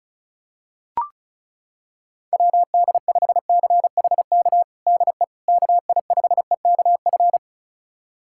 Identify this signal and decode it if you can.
Morse code, heard as a keyed tone, about 36 wpm, 700 Hz. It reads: WD5CHK DE KI5EKF
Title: USA Callsign DE Callsign 35wpm